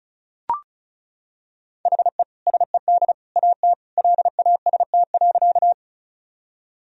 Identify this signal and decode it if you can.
Morse code, heard as a keyed tone, about 36 wpm, 700 Hz. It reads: HE SED AT LAST.
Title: Sentences from Top 300 Words and Abbreviations 35wpm